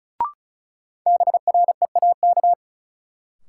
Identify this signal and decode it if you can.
Morse code, heard as a keyed tone, about 34 wpm, 700 Hz. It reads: BREAK
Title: Sentences from Top 900 Words - New Word Drill - 35wpm